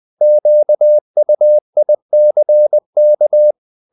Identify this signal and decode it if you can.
Morse code, heard as a keyed tone, about 20 wpm, 600 Hz. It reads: QUICK
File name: the old man and the sea / the old man and the sea part 1 20wpm